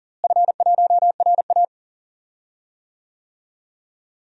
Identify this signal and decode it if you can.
Morse code, heard as a keyed tone, about 40 wpm, 700 Hz. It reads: F1RA
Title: DXCC-100 Call Signs - 4 Characters - Repeated 3x - 40wpm